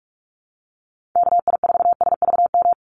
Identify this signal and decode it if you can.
Morse code, heard as a keyed tone, about 44 wpm, 700 Hz. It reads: KI4SVM
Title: NA SOTA Chaser Callsigns 45wpm